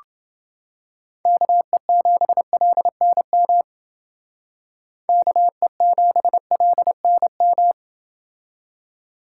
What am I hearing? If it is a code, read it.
Morse code, heard as a keyed tone, about 28 wpm, 700 Hz. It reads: KE7LNM KE7LNM